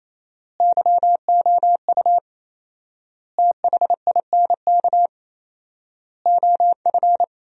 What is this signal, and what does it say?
Morse code, heard as a keyed tone, about 28 wpm, 700 Hz. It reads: YOU THINK OF
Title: 4x Word-Spacing - Sentences from Top 200 Words - 28wpm